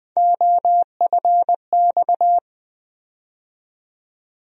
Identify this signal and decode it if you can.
Morse code, heard as a keyed tone, about 20 wpm, 700 Hz. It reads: OFX
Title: USA Call Sign Postfixes - 3 Letters 20wpm